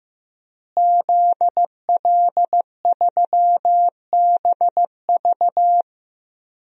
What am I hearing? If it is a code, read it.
Morse code, heard as a keyed tone, about 15 wpm, 700 Hz. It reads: ZL3BV
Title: DXCC-100 Call Signs - 5 Characters - Repeated 3x - 15wpm